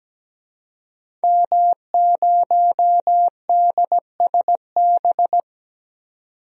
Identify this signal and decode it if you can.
Morse code, heard as a keyed tone, about 17 wpm, 700 Hz. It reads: M0DSB
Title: DXCC-100 Call Signs - 5 Characters - 17wpm